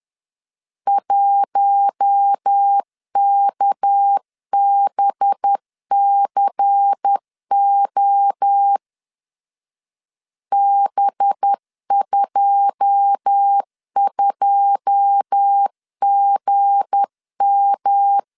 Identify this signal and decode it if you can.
Morse code, heard as a keyed tone, about 10 wpm, 800 Hz. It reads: 1KBCO B22GM
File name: CW / DET / 1 9 1500C S B 10wpm 4439